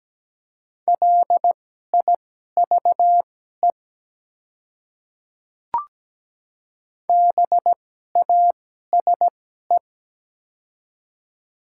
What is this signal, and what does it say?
Morse code, heard as a keyed tone, about 17 wpm, spaced as for 12 wpm, 700 Hz. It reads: LIVE BASE
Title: 2x Character Spacing - Top 300 Words - 17wpm